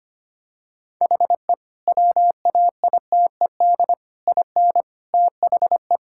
Morse code, heard as a keyed tone, about 25 wpm, 700 Hz. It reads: HE WAITED IN THE
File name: practice / 25wpm / Sentences with 4-Word Phrases - 25wpm